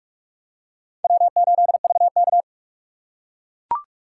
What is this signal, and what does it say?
Morse code, heard as a keyed tone, about 44 wpm, 700 Hz. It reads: W8VK